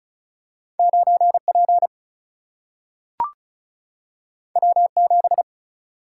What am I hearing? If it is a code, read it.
Morse code, heard as a keyed tone, about 36 wpm, 700 Hz. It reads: 9P W7
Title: Single-Letter Single-Number 35wpm